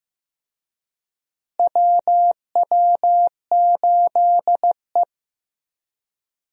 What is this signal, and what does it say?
Morse code, heard as a keyed tone, about 16 wpm, 700 Hz. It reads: WW8E
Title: USA Call Signs - 4 Characters - Repeated 3x - 15wpm